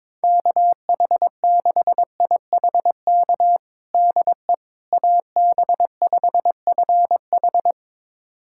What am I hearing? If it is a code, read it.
Morse code, heard as a keyed tone, about 22 wpm, 700 Hz. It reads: KH6IHK DE AB5FH
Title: USA Callsign DE Callsign 22wpm